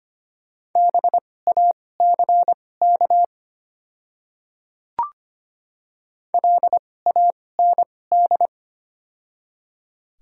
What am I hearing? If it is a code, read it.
Morse code, heard as a keyed tone, about 25 wpm, spaced as for 18 wpm, 700 Hz. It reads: BACK LAND
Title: 2x Character Spacing - Top 200 Words - 25wpm